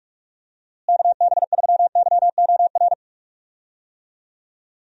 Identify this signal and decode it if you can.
Morse code, heard as a keyed tone, about 44 wpm, 700 Hz. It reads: KB3YOR
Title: USA Call Signs - 6 Characters - Repeated 2x - 45wpm